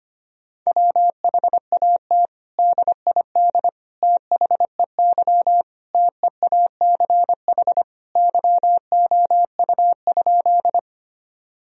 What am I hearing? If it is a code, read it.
Morse code, heard as a keyed tone, about 25 wpm, 700 Hz. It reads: WHAT DID THEY TEACH YOU?
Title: Sentences from Top 1000 Words - Review Part 2 - 25wpm